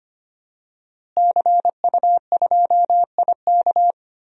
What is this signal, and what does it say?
Morse code, heard as a keyed tone, about 25 wpm, 700 Hz. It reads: CU2IK
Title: DXCC-100 Call Signs - 5 Characters - Repeated 2x - 25wpm